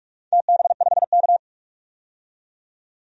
Morse code, heard as a keyed tone, about 47 wpm, 700 Hz. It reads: TB5K
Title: DXCC-100 Call Signs - 4 Characters - 45wpm